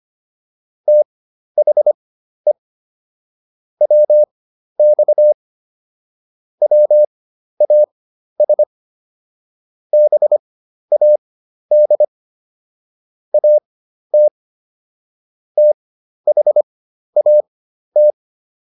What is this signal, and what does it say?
Morse code, heard as a keyed tone, about 25 wpm, spaced as for 12 wpm, 600 Hz. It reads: THE WX WAS BAD AT THAT
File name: SS017 12